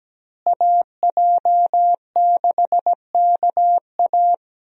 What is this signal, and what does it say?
Morse code, heard as a keyed tone, about 17 wpm, 700 Hz. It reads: AJ6KA
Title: USA Call Signs - 5 Characters - Repeated 2x - 17wpm